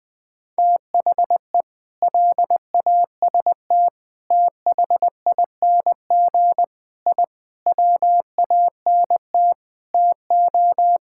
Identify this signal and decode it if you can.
Morse code, heard as a keyed tone, about 19 wpm, 700 Hz. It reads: THE LAST THING I WANT TO
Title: Sentences with 7-Word Phrases - Spoken Beginning - 20wpm